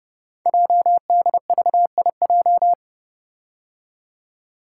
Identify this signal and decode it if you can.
Morse code, heard as a keyed tone, about 30 wpm, 700 Hz. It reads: JDVIJ
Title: Code Groups - 5 Letter:Numbers - 30wpm